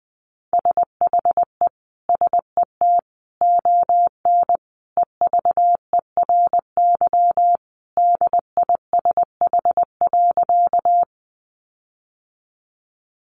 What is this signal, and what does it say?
Morse code, heard as a keyed tone, about 20 wpm, 700 Hz. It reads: SHE SET ON EVERY DISH.
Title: Sentences from Top 1300 Words 20wpm